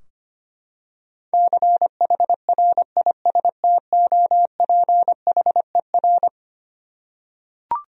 Morse code, heard as a keyed tone, about 25 wpm, 700 Hz. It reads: CHRISTOPHER
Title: USA Common Names - 25wpm